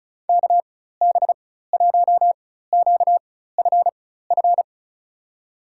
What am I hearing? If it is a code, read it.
Morse code, heard as a keyed tone, about 36 wpm, spaced as for 16 wpm, 700 Hz. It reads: KB1QFF
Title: USA Call Signs - 6 Characters - 4x Character Spacing - 35wpm